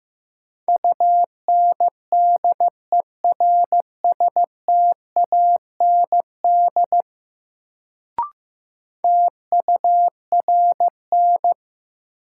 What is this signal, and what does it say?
Morse code, heard as a keyed tone, about 15 wpm, 700 Hz. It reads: UNDERSTAND TURN
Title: Sets of 4 Words - New Word Drill - 15wpm